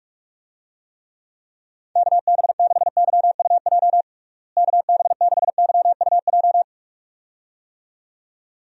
Morse code, heard as a keyed tone, about 46 wpm, 700 Hz. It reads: KB6YUJ KB6YUJ
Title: USA Call Signs - 6 Characters - Repeated 2x - 45wpm